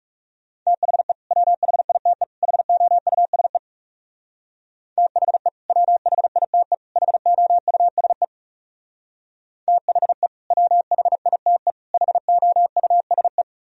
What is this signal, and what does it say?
Morse code, heard as a keyed tone, about 36 wpm, 700 Hz. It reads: THE WHITE HOUSE THE WHITE HOUSE THE WHITE HOUSE
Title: Speed-Racing - Sets of 3 Words 30wpm